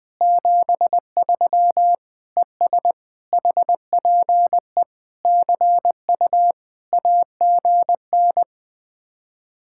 Morse code, heard as a keyed tone, about 20 wpm, 700 Hz. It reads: 73 ES HPE CU AGN
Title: Speed-Racing Common QSO Phrases - 17wpm